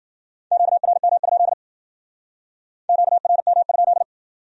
Morse code, heard as a keyed tone, about 53 wpm, 700 Hz. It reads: QRM? QRM?